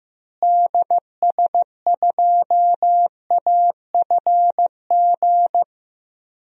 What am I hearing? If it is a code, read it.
Morse code, heard as a keyed tone, about 15 wpm, 700 Hz. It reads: DS2AFG
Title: DXCC-100 Call Signs - 6 Characters - Repeated 2x - 15wpm